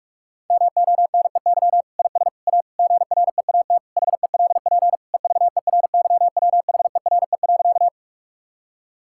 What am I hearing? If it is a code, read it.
Morse code, heard as a keyed tone, about 46 wpm, 700 Hz. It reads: MONEY IS A GREAT HELP EVERYWHERE.
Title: Sentences from Top 1600 Words 45wpm